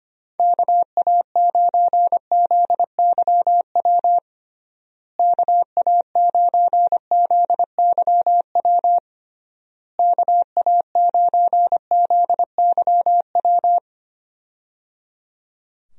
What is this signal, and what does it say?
Morse code, heard as a keyed tone, about 25 wpm, 700 Hz. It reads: KA9ZYW KA9ZYW KA9ZYW
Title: USA Call Signs - 6 Characters - Repeated 3x - 25wpm